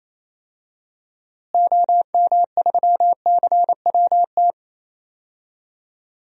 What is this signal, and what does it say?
Morse code, heard as a keyed tone, about 28 wpm, 700 Hz. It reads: OM3CWT
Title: DXCC-100 Call Signs - 6 Characters - Repeated 2x - 28wpm